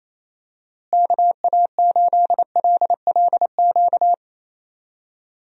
Morse code, heard as a keyed tone, about 28 wpm, 700 Hz. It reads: KA8LLQ
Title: USA Call Signs - 6 Characters - Repeated 3x - 28wpm